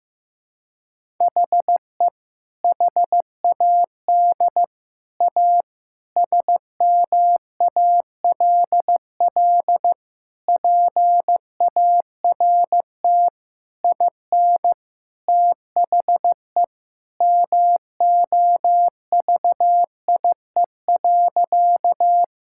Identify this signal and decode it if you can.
Morse code, heard as a keyed tone, about 15 wpm, 700 Hz. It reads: HE HAD A SMALL PART IN THE MOVIE.
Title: Sentences from Top 1000 Words - Review Part 5 - 15wpm